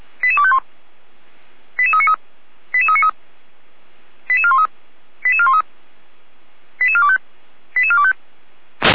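Five-tone selective call (ZVEI), heard as quick runs of five tones, tones of about 70 ms. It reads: 88251 88393 88393 88413 88413 88425 88425
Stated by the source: ZVEI five-tone selective call 88343 received by SDR.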